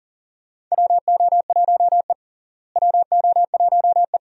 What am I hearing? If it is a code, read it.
Morse code, heard as a keyed tone, about 40 wpm, 700 Hz. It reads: WO1E WO1E
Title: USA Call Signs - 4 Characters - Repeated 2x - 40wpm